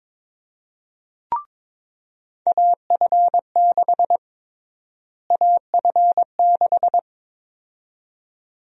Morse code, heard as a keyed tone, about 22 wpm, 700 Hz. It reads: AF6 AF6
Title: USA Call Sign Prefixes - Repeated 2x - 22wpm